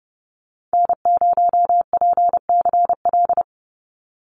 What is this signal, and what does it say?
Morse code, heard as a keyed tone, about 30 wpm, 700 Hz. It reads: N0PCL
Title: NA SOTA Activator Callsigns 30wpm